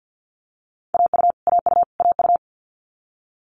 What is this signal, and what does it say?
Morse code, heard as a keyed tone, about 50 wpm, 700 Hz. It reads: AU AU AU
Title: Multi-letter Phonemes Repeated 3x 50wpm